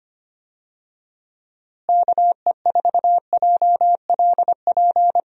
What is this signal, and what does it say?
Morse code, heard as a keyed tone, about 25 wpm, 700 Hz. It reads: KE4JLP
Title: USA Call Signs - 6 Characters - Repeated 3x - 25wpm